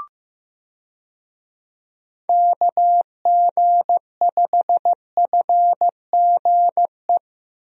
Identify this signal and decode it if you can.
Morse code, heard as a keyed tone, about 15 wpm, 700 Hz. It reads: KG5FGE
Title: USA Call Signs - 6 Characters - Repeated 2x - 15wpm